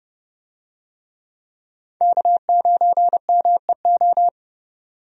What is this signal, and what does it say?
Morse code, heard as a keyed tone, about 30 wpm, 700 Hz. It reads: K9MEO
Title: USA Call Signs - 5 Characters - 30wpm